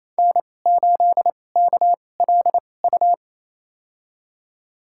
Morse code, heard as a keyed tone, about 28 wpm, spaced as for 20 wpm, 700 Hz. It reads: N8KLU
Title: USA Call Signs - 2x Character Spacing - 28wpm